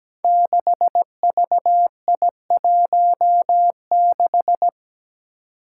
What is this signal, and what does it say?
Morse code, heard as a keyed tone, about 17 wpm, 700 Hz. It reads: 6VI16
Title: Five Challenging Letter-Number Combinations 17wpm